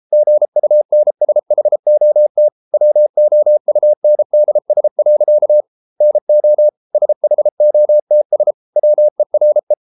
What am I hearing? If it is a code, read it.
Morse code, heard as a keyed tone, about 33 wpm, 600 Hz. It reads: GUNSHOT WOUNDS. NO SHOTS WERE